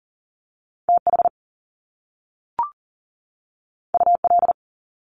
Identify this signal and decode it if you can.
Morse code, heard as a keyed tone, about 39 wpm, 700 Hz. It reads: TH UL
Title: Top Two Letter Combinations in Words 40wpm